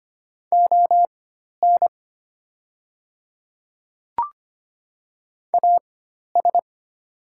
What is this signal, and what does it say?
Morse code, heard as a keyed tone, about 25 wpm, spaced as for 12 wpm, 700 Hz. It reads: ON AS
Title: 4x Character Spacing - Top 200 Words - 25wpm